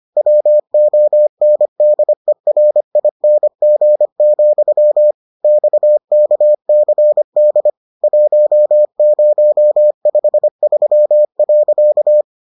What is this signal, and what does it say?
Morse code, heard as a keyed tone, about 25 wpm, 600 Hz. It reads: WONDERING, XKCD 1053.